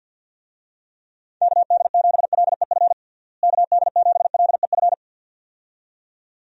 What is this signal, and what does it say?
Morse code, heard as a keyed tone, about 52 wpm, 700 Hz. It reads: KD7LEF KD7LEF
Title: USA Call Signs - 6 Characters - Repeated 2x - 50wpm